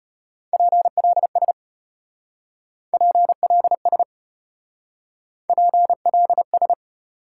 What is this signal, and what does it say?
Morse code, heard as a keyed tone, about 33 wpm, 700 Hz. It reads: PLS PLS PLS